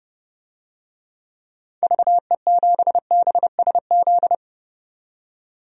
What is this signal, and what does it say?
Morse code, heard as a keyed tone, about 29 wpm, 700 Hz. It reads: VE7BSZ
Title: DXCC-100 Call Signs - 6 Characters - 30wpm